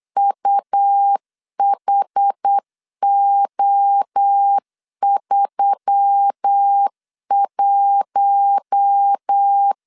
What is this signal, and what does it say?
Morse code, heard as a keyed tone, about 8 wpm, 800 Hz. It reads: UHO31
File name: LEZIONI / 6 PKX0 / 6 P 8wpm 3025